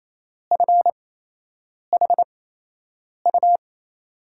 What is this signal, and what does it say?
Morse code, heard as a keyed tone, about 28 wpm, spaced as for 8 wpm, 700 Hz. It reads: FHU
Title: USA Call Signs - 6 Characters - 8x Character Spacing - 28wpm